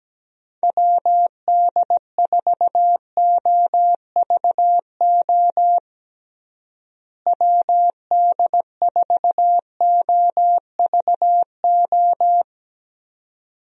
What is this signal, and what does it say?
Morse code, heard as a keyed tone, about 17 wpm, 700 Hz. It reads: WD4OVO WD4OVO
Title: USA Call Signs - 6 Characters - Repeated 2x - 17wpm